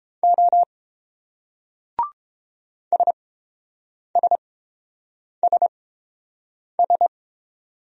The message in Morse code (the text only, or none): O SSSS